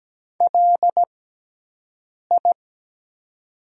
Morse code, heard as a keyed tone, about 17 wpm, spaced as for 6 wpm, 700 Hz. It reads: LI
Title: 6x Character Spacing - Top 300 Words - 17wpm